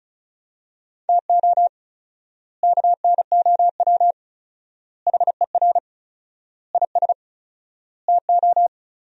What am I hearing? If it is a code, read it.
Morse code, heard as a keyed tone, about 34 wpm, 700 Hz. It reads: TO KNOW HER IS TO
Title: 4x Word-Spacing - Sentences from Top 400 Words - 35wpm